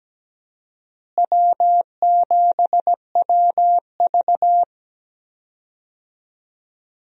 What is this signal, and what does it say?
Morse code, heard as a keyed tone, about 17 wpm, 700 Hz. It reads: W7WV
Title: USA Call Signs - 4 Characters - Repeated 3x - 17wpm